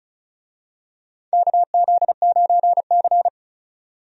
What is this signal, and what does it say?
Morse code, heard as a keyed tone, about 36 wpm, 700 Hz. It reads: KZ9C
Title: USA Call Signs - 4 Characters - 35wpm